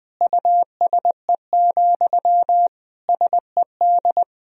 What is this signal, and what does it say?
Morse code, heard as a keyed tone, about 21 wpm, 700 Hz. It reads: USE, SED
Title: Sentences from Top 500 Words and Abbreviations 20wpm